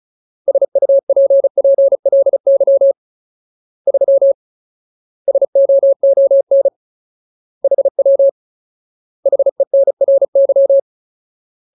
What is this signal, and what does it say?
Morse code, heard as a keyed tone, about 34 wpm, 550 Hz. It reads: SUPPLY 3 SOON HW HENRY
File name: Ultimate Word List / 35WPM Ultimate